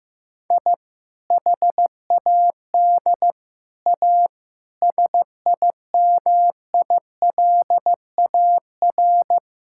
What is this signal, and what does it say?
Morse code, heard as a keyed tone, about 15 wpm, 690 Hz. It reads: I HAD A SIMILAR
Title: Sentences from Top 1000 Words - Review Part 5 - 15wpm